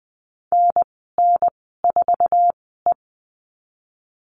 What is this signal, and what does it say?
Morse code, heard as a keyed tone, about 20 wpm, spaced as for 14 wpm, 700 Hz. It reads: NN4E
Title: 069 Lesson 12g - Letter d - Call Signs 20wpm